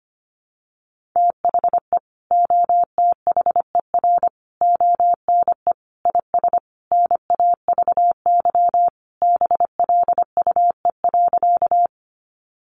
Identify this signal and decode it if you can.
Morse code, heard as a keyed tone, about 25 wpm, 700 Hz. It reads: THE OTHER ONE IS NAVY BLUE.
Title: Additional Easy Phrases 25wpm